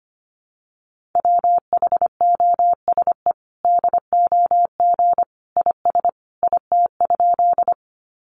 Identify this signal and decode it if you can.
Morse code, heard as a keyed tone, about 25 wpm, 700 Hz. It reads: WHOSE DOG IS IT?
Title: Sentences from Top 1200 Words 25wpm